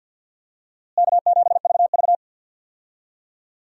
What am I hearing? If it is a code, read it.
Morse code, heard as a keyed tone, about 52 wpm, 700 Hz. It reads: K7VV
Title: USA Call Signs - 4 Characters - Repeated 3x - 50wpm